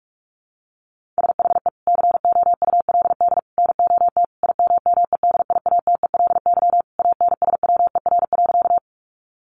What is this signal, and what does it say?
Morse code, heard as a keyed tone, about 44 wpm, 700 Hz. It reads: SHE COULD NOT IMMEDIATELY ANSWER.
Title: Sentences from Top 1100 Words 45wpm